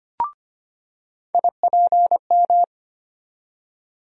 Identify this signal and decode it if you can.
Morse code, heard as a keyed tone, about 25 wpm, 700 Hz. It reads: IPM
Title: USA Call Sign Postfixes - 3 Letters 25wpm